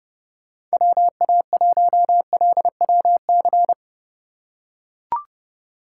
Morse code, heard as a keyed tone, about 32 wpm, 700 Hz. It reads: WA1LWC
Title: K1USN SST QSO Practice - 30wpm